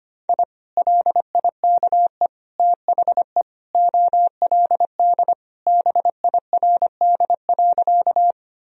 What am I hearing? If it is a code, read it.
Morse code, heard as a keyed tone, about 25 wpm, 700 Hz. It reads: I LIKE THE OLD BIRD.
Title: Sentences from Top 1000 Words - Review Part 2 - 25wpm